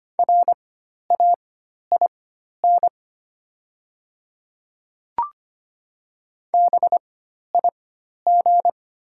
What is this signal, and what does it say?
Morse code, heard as a keyed tone, about 25 wpm, spaced as for 12 wpm, 700 Hz. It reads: RAIN BIG
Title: 4x Character Spacing - Top 500 Words - 25wpm